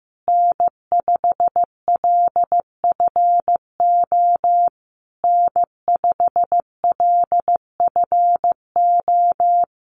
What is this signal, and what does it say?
Morse code, heard as a keyed tone, about 15 wpm, 700 Hz. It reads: N5LFO N5LFO
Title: USA Call Signs Repeated 3x 15wpm